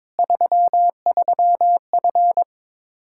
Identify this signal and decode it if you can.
Morse code, heard as a keyed tone, about 21 wpm, 700 Hz. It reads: 33F
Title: K1USN SST QSO Practice 22wpm